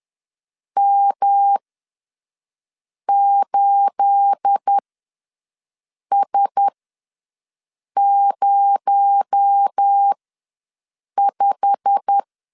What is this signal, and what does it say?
Morse code, heard as a keyed tone, about 10 wpm, spaced as for 5 wpm, 800 Hz. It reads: M8S05
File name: LEZIONI / 9 28 ws 10wpm 4124